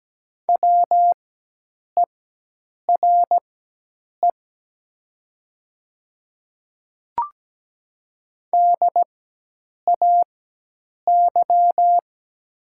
Morse code, heard as a keyed tone, about 17 wpm, spaced as for 8 wpm, 700 Hz. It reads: WERE DAY
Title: 4x Character Spacing - Top 100 Words - 17wpm